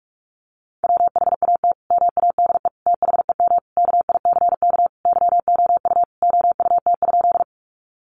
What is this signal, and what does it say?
Morse code, heard as a keyed tone, about 44 wpm, 700 Hz. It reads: WHAT MADE THEM KICK YOU OUT?
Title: Sentences from Top 1300 Words 45wpm